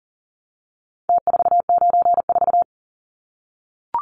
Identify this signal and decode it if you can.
Morse code, heard as a keyed tone, about 40 wpm, 700 Hz. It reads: T494
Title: Numbers - Contest Style - 4-Digit 40wpm